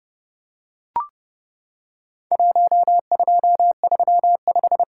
Morse code, heard as a keyed tone, about 32 wpm, 700 Hz. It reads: 1235
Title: Worked All Europe DX Contest - QTC Exchange - 30wpm